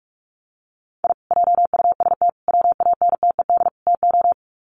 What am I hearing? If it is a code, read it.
Morse code, heard as a keyed tone, about 44 wpm, 700 Hz. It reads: I JUST WANTED TO
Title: Sets of 4 Words 45wpm